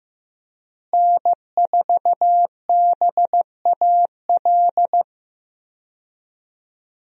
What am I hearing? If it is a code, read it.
Morse code, heard as a keyed tone, about 15 wpm, 700 Hz. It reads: N4BAL
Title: USA Call Signs - 5 Characters - Repeated 3x - 15wpm